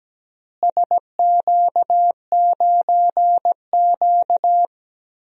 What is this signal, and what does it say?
Morse code, heard as a keyed tone, about 17 wpm, 700 Hz. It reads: SQ9Q
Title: DXCC-100 Call Signs - 4 Characters - Repeated 3x - 17wpm